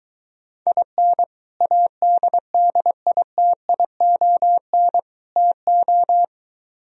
Morse code, heard as a keyed tone, about 23 wpm, 700 Hz. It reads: IN ADDITION TO